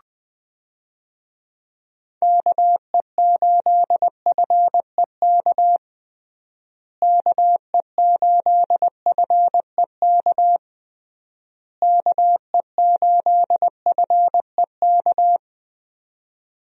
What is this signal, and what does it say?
Morse code, heard as a keyed tone, about 21 wpm, 700 Hz. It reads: KE8FEK KE8FEK KE8FEK
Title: USA Call Signs - 6 Characters - Repeated 3x - 20wpm